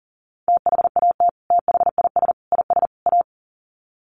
Morse code, heard as a keyed tone, about 40 wpm, 700 Hz. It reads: THAT THIS IS A
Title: Sets of 4 Words 40wpm